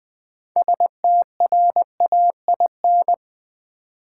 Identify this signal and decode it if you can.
Morse code, heard as a keyed tone, about 20 wpm, 700 Hz. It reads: STRAIN
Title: 6 Letter Words - First 3 Letters Given - 20wpm